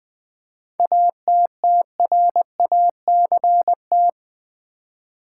Morse code, heard as a keyed tone, about 20 wpm, 700 Hz. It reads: ATTRACT
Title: Seven Letter Words - 20wpm